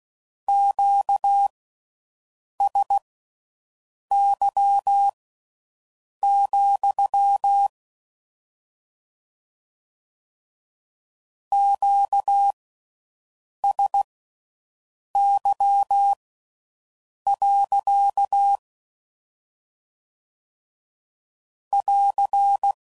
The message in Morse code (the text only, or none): QSY, QSY. <AR>